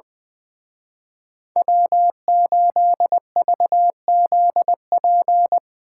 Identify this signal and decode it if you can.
Morse code, heard as a keyed tone, about 22 wpm, 700 Hz. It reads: W8VZP
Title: Speed Racing - USA 1x3 Call Signs 20wpm